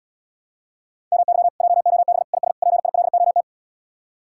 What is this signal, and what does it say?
Morse code, heard as a keyed tone, about 80 wpm, 700 Hz. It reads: MY JOB IS PILOT